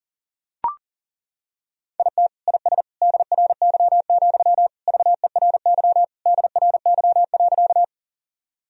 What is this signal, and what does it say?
Morse code, heard as a keyed tone, about 40 wpm, 700 Hz. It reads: IT IS DRY, VERY DRY.
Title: Sentences from Top 1000 Words - Review Part 3 - 40wpm